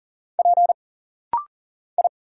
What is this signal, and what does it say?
Morse code, heard as a keyed tone, about 41 wpm, 700 Hz. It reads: P I
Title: Single Letters - Rapid-Fire 40wpm